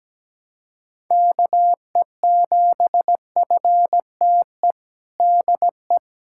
Morse code, heard as a keyed tone, about 17 wpm, 700 Hz. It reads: KE7FTE DE
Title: USA Callsign DE Callsign - 6 Characters - 17wpm